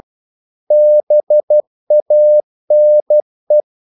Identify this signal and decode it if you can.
Morse code, heard as a keyed tone, about 12 wpm, 600 Hz. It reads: BANE
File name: Korte ord DK 12wpm